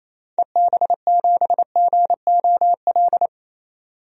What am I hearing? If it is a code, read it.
Morse code, heard as a keyed tone, about 28 wpm, 700 Hz. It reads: EB7GOL
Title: DXCC-100 Call Signs - 6 Characters - 28wpm